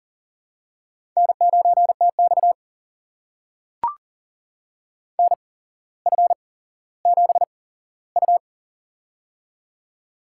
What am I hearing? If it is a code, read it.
Morse code, heard as a keyed tone, about 40 wpm, spaced as for 17 wpm, 700 Hz. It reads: N9TX NF7U
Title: USA Call Signs - 4 Characters - 8x Character Spacing - 40wpm